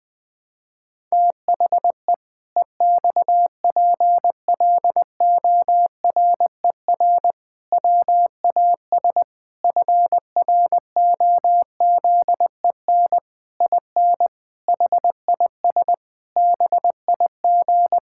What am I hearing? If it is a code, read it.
Morse code, heard as a keyed tone, about 20 wpm, 700 Hz. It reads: THE EXPLORER WAS FROZEN IN HIS BIG
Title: Pangram Challenge v2 - 20wpm